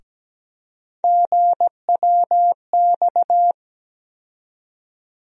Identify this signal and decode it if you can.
Morse code, heard as a keyed tone, about 17 wpm, 700 Hz. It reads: GWX
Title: USA Call Sign Postfixes - 3 Letters 17wpm